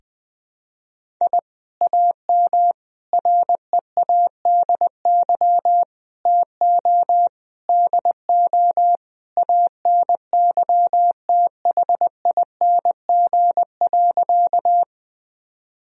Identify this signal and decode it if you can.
Morse code, heard as a keyed tone, about 20 wpm, 700 Hz. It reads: I AM READY TO DO ANYTHING.
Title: Sentences from Top 1000 Words - Review Part 2 - 20wpm